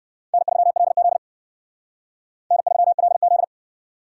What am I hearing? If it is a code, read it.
Morse code, heard as a keyed tone, about 68 wpm, 700 Hz. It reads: N3LZ N3LZ